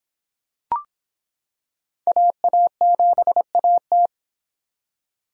Morse code, heard as a keyed tone, about 26 wpm, 700 Hz. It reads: AA7AT